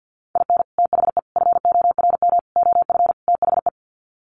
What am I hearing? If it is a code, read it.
Morse code, heard as a keyed tone, about 50 wpm, 700 Hz. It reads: IN THE FORM OF THE
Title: Sets of 5 Words Encore2 50wpm